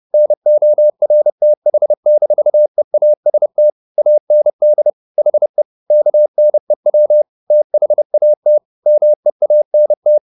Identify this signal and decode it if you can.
Morse code, heard as a keyed tone, about 30 wpm, 600 Hz. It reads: NORTH-EAST AND HE KNEW THAT MEANT